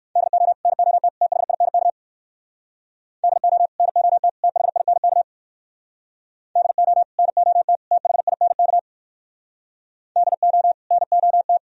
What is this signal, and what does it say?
Morse code, heard as a keyed tone, about 65 wpm, 700 Hz. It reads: DO NOT THINK DO NOT THINK DO NOT THINK DO NOT